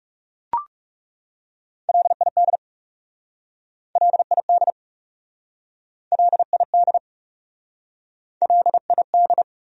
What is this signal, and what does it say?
Morse code, heard as a keyed tone, about 40 wpm, 700 Hz. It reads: LID LID LID LID